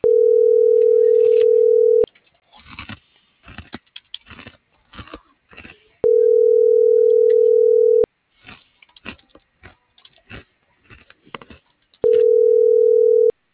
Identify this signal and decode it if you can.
Ringback tone (North America), 440 + 480 Hz.